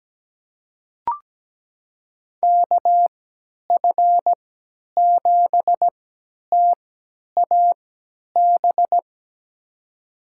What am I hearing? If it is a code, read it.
Morse code, heard as a keyed tone, about 17 wpm, spaced as for 10 wpm, 700 Hz. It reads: KF7TAB
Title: USA Call Signs - 6 Characters - 3x Character Spacing - 17wpm